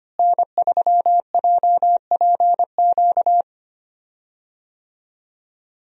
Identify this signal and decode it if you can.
Morse code, heard as a keyed tone, about 25 wpm, 700 Hz. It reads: N3JPQ
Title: USA Call Signs - 5 Characters - Repeated 2x - 25wpm